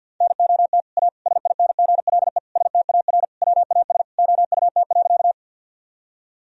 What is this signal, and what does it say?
Morse code, heard as a keyed tone, about 50 wpm, 700 Hz. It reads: NOT A SINGLE STAR WAS OUT.
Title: Sentences from Top 1000 Words - Review Part 4 - 50wpm